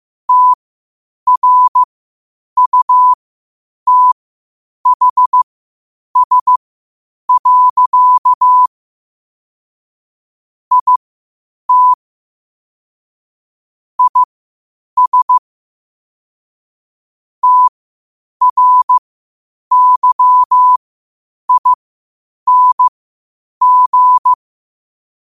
Morse code, heard as a keyed tone, about 15 wpm, spaced as for 8 wpm, 1000 Hz. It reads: TRUTHS. IT IS TRYING
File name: qotdfiles / 8wpm